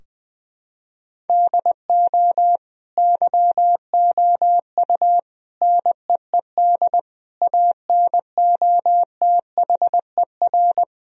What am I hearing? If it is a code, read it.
Morse code, heard as a keyed tone, about 20 wpm, 700 Hz. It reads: DO YOU NEED ANOTHER
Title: Sentences from Top 1000 Words - Review Part 4 - 20wpm